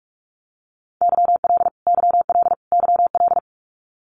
Morse code, heard as a keyed tone, about 44 wpm, 700 Hz. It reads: YL YL YL
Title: Abbreviations Repeated 3x - 45wpm